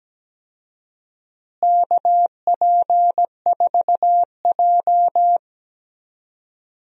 Morse code, heard as a keyed tone, about 17 wpm, 700 Hz. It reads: KP4J
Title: DXCC-100 Call Signs - 4 Characters - Repeated 2x - 17wpm